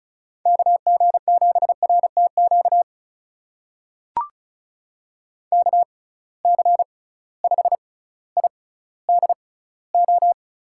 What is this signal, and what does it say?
Morse code, heard as a keyed tone, about 36 wpm, spaced as for 18 wpm, 690 Hz. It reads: KG7RTQ KC5IDO
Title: USA Call Signs - 6 Characters - 6x Character Spacing - 35wpm